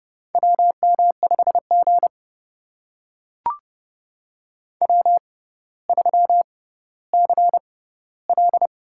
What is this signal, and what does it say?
Morse code, heard as a keyed tone, about 30 wpm, spaced as for 15 wpm, 690 Hz. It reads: WM5G W3CL
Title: USA Call Signs - 4 Characters - 6x Character Spacing - 30wpm